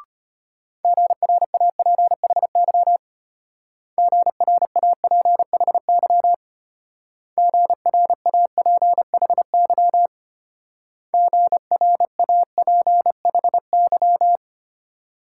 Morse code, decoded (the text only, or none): GRAPHY GRAPHY GRAPHY GRAPHY